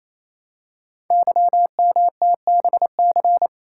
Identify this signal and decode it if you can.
Morse code, heard as a keyed tone, about 27 wpm, 700 Hz. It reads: YMTBC
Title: Code Groups - 5 Letters - 28wpm